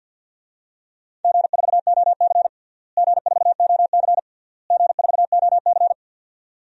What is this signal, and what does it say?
Morse code, heard as a keyed tone, about 51 wpm, 700 Hz. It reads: G4OC G4OC G4OC
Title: DXCC-100 Call Signs - 4 Characters - Repeated 3x - 50wpm